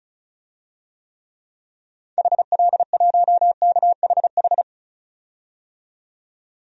Morse code, heard as a keyed tone, about 36 wpm, 700 Hz. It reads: HL1KHH
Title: DXCC-100 Call Signs - 6 Characters - 35wpm